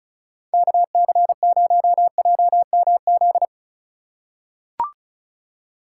Morse code, heard as a keyed tone, about 34 wpm, 700 Hz. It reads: KC0JMZ
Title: USA Call Signs - 6 Characters - 2x Character Spacing - 35wpm